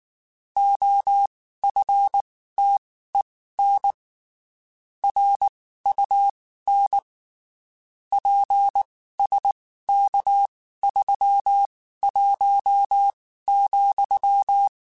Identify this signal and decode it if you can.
Morse code, heard as a keyed tone, about 19 wpm, spaced as for 14 wpm, 800 Hz. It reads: OFTEN RUN PSK31,
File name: (spaced) / QSOs Spaced-mp3s / kc7sac @18 sp